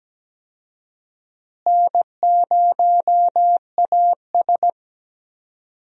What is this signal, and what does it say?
Morse code, heard as a keyed tone, about 17 wpm, 700 Hz. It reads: N0AS
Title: USA Call Signs - 4 Characters - Repeated 2x - 17wpm